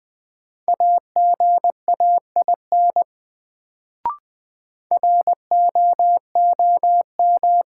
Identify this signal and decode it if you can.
Morse code, heard as a keyed tone, about 21 wpm, 700 Hz. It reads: AGAIN ROOM
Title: Sets of 4 Words - New Word Drill - 20wpm